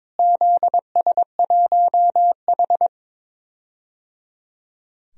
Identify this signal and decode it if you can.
Morse code, heard as a keyed tone, about 22 wpm, 700 Hz. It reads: ZS1H
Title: DXCC-100 Call Signs - 4 Characters - Repeated 3x - 22wpm